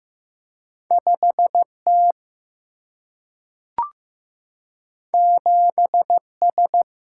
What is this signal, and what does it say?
Morse code, heard as a keyed tone, about 15 wpm, 700 Hz. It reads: 5T 7S
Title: Single-Letter Single-Number 15wpm